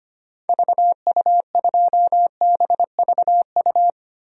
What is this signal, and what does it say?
Morse code, heard as a keyed tone, about 25 wpm, 700 Hz. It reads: VU2BVU
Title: DXCC-100 Call Signs - 6 Characters - Repeated 3x - 25wpm